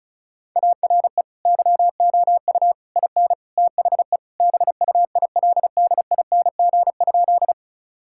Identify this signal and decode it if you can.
Morse code, heard as a keyed tone, about 36 wpm, 700 Hz. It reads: ARE YOU IN THE BUILDING?
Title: Sentences from Top 1000 Words - Review Part 5 - 35wpm